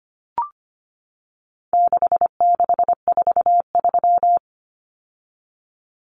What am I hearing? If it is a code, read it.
Morse code, heard as a keyed tone, about 25 wpm, 700 Hz. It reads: 6643
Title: Common Cut Numbers - 4-Digit 25wpm